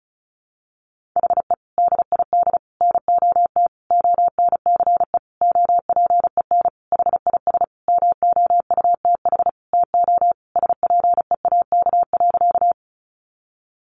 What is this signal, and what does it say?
Morse code, heard as a keyed tone, about 34 wpm, 700 Hz. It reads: HE DID NOT ONCE OPEN HIS MOUTH TO SPEAK.
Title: Sentences from Top 1500 Words 35wpm